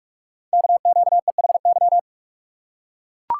Morse code, heard as a keyed tone, about 44 wpm, 700 Hz. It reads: KQEHY
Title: Code Groups - 5 Letters - 45wpm